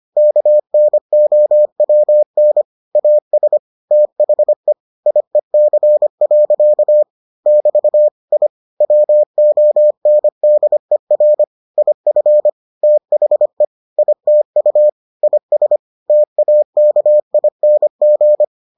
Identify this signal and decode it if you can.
Morse code, heard as a keyed tone, about 25 wpm, 600 Hz. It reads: KNOWN AS THE IEC. <BT> I WONDER IF THE ITU IS TAKING